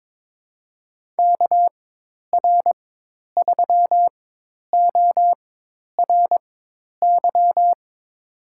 Morse code, heard as a keyed tone, about 22 wpm, spaced as for 10 wpm, 700 Hz. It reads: KR3ORY
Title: USA Call Signs - 4x Character Spacing - 22wpm